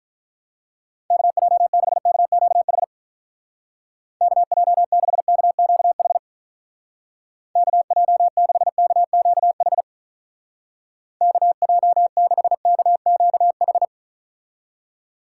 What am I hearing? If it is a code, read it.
Morse code, heard as a keyed tone, about 45 wpm, 700 Hz. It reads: KJ6KQH KJ6KQH KJ6KQH KJ6KQH